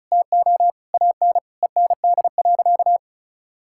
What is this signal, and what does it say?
Morse code, heard as a keyed tone, about 36 wpm, 700 Hz. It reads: TO AN END.
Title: Sentences from Top 1000 Words - Review Part 5 - 35wpm